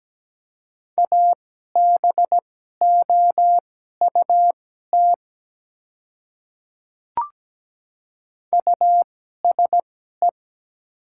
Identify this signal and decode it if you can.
Morse code, heard as a keyed tone, about 17 wpm, spaced as for 12 wpm, 700 Hz. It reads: ABOUT USE
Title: 2x Character Spacing - Top 100 Words - 17wpm